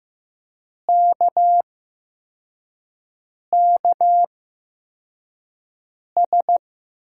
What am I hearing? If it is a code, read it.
Morse code, heard as a keyed tone, about 15 wpm, spaced as for 4 wpm, 700 Hz. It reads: KKS
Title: USA Call Signs - 8x Character Spacing - 15wpm